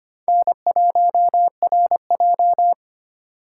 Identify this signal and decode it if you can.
Morse code, heard as a keyed tone, about 25 wpm, 700 Hz. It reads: N1RJ
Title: USA Call Signs - 4 Characters - 3x Character Spacing - 25wpm